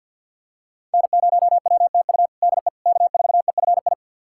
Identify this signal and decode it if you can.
Morse code, heard as a keyed tone, about 48 wpm, 700 Hz. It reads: N0WTU DE K4EFI
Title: USA Callsign DE Callsign - 5 Characters - 50wpm